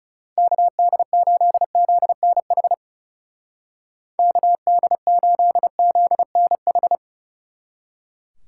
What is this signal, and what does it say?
Morse code, heard as a keyed tone, about 37 wpm, 700 Hz. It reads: KD8ZNH KD8ZNH
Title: Speed Racing - USA 2x3 Call Signs 30wpm